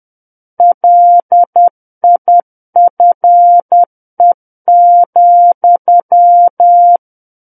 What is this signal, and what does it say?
Morse code, heard as a keyed tone, about 10 wpm, 700 Hz. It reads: LIFE,
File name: Book One / 01-Chapter 1 - 10 WPM